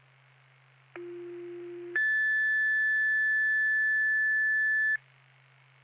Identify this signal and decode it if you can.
Two-tone page, heard as a one-second tone, then a three-second tone, about 350 then 1750 Hz. It is heard from 1.0 to 5.0 s.